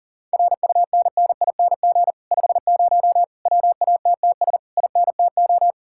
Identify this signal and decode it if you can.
Morse code, heard as a keyed tone, about 52 wpm, 700 Hz. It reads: RUNNING 50 WATTS INTO